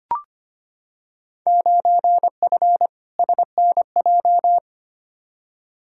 Morse code, heard as a keyed tone, about 25 wpm, 700 Hz. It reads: 9F SNJ
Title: Field Day Exchanges Encore - 25wpm